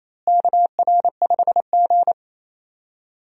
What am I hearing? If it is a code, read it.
Morse code, heard as a keyed tone, about 28 wpm, 700 Hz. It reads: KR5G
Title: USA Call Signs - 4 Characters - Repeated 2x - 28wpm